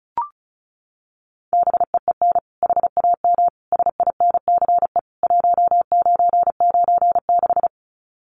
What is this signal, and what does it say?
Morse code, heard as a keyed tone, about 34 wpm, 700 Hz. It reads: BEEN HAM SINCE 1996
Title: Common QSO Phrases 35wpm